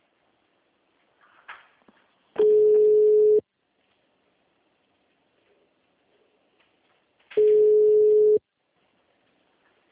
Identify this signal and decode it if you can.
Ringback tone (Europe, 425 Hz).